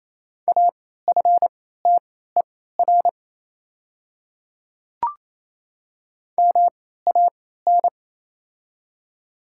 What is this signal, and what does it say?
Morse code, heard as a keyed tone, about 28 wpm, spaced as for 16 wpm, 700 Hz. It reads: AFTER MAN
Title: 3x Character Spacing - Top 300 Words - 28wpm